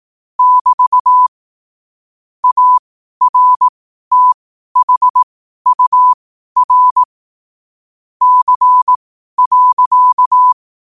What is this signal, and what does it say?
Morse code, heard as a keyed tone, about 18 wpm, spaced as for 12 wpm, 1000 Hz. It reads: <BT> ARTHUR C.
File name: qotdfiles / 12wpm